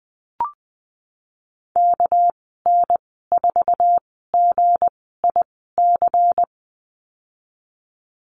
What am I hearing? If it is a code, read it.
Morse code, heard as a keyed tone, about 19 wpm, spaced as for 14 wpm, 700 Hz. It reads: KN4GIC
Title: 207 Lesson 33g - Letter j - Call Signs 20wpm